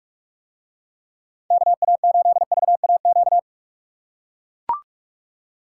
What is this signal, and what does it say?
Morse code, heard as a keyed tone, about 44 wpm, 700 Hz. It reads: KA8VAQ